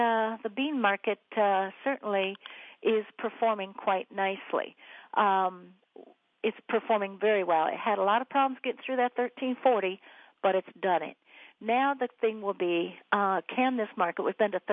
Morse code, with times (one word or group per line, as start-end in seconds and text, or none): none